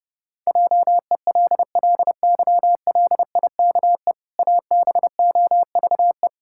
Morse code, heard as a keyed tone, about 30 wpm, 700 Hz. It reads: JELLYLIKE ABOVE
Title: Pangram Challenge v2 - 30wpm